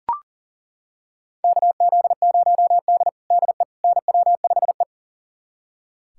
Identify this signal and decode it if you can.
Morse code, heard as a keyed tone, about 40 wpm, 700 Hz. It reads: KZ0D DE NW5E
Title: USA Callsign DE Callsign - 4 Characters - 40wpm